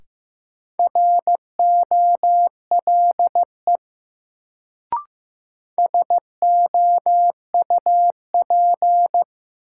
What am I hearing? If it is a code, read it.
Morse code, heard as a keyed tone, about 15 wpm, 700 Hz. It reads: ROLE SOUP
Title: Sentences from Top 1000 Words - Review Part 6 - New Word Drill - 15wpm